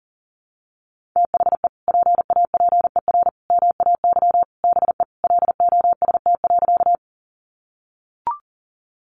Morse code, heard as a keyed tone, about 40 wpm, 700 Hz. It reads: THE PAPER MAY BE LOST.
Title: Sentences from Top 1400 Words 40wpm